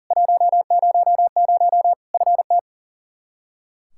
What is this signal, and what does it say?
Morse code, heard as a keyed tone, about 40 wpm, 700 Hz. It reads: 100 FT
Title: Practice QSO - Exchange 2 - Info - 40wpm